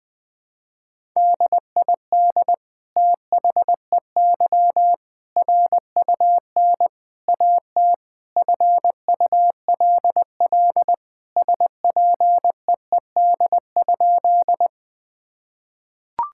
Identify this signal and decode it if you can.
Morse code, heard as a keyed tone, about 20 wpm, 700 Hz. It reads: DID THEY RUN AT FULL SPEED?
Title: Sentences from Top 1000 Words - Review Part 5 - 20wpm